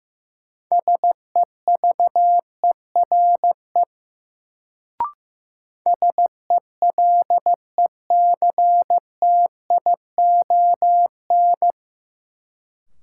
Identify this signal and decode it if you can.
Morse code, heard as a keyed tone, about 15 wpm, 700 Hz. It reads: SEVERE SELECTION
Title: Sentences from Top 1400 Words - New Word Drill - 15wpm